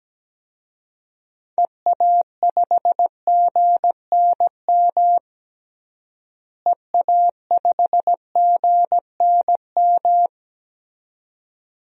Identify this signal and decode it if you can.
Morse code, heard as a keyed tone, about 17 wpm, 700 Hz. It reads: EA5GNM EA5GNM
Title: DXCC-100 Call Signs - 6 Characters - Repeated 2x - 17wpm